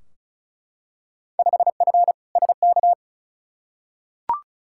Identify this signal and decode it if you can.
Morse code, heard as a keyed tone, about 36 wpm, 700 Hz. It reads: 5F SK
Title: Field Day Exchanges Encore - 35wpm